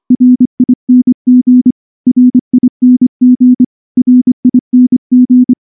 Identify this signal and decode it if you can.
Morse code, heard as a keyed tone, about 24 wpm, 250 Hz. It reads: RING RING RING
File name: morse / morse-training-25WPM